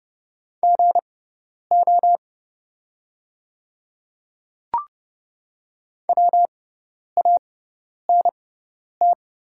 Morse code, heard as a keyed tone, about 30 wpm, spaced as for 10 wpm, 690 Hz. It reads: GO WANT
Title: 6x Character Spacing - Top 200 Words - 30wpm